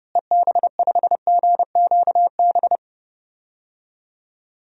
Morse code, heard as a keyed tone, about 30 wpm, 700 Hz. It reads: EB5GQB
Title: DXCC-100 Call Signs - 6 Characters - Repeated 3x - 30wpm